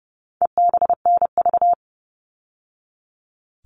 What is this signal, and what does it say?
Morse code, heard as a keyed tone, about 28 wpm, 700 Hz. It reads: EBNV
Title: All Cut Numbers - 4-Digit 30wpm